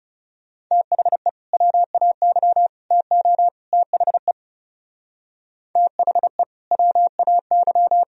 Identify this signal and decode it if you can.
Morse code, heard as a keyed tone, about 37 wpm, 700 Hz. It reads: THE WAY TO THE THE WAY
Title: Speed-Racing - Sets of 4 Words 30wpm